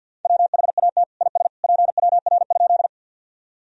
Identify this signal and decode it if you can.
Morse code, heard as a keyed tone, about 52 wpm, 700 Hz. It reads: WHAT IS PWR?
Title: Sentences from Top 500 Words and Abbreviations 50wpm